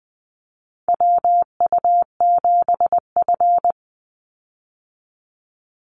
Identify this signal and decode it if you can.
Morse code, heard as a keyed tone, about 20 wpm, 700 Hz. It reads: WU7F
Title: Call Signs - 20wpm